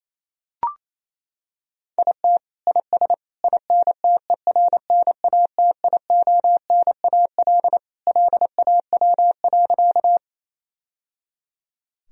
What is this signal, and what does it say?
Morse code, heard as a keyed tone, about 28 wpm, 700 Hz. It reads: IT IS INTERNATIONAL LAW.
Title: Sentences from Top 1000 Words - Review Part 4 - 28wpm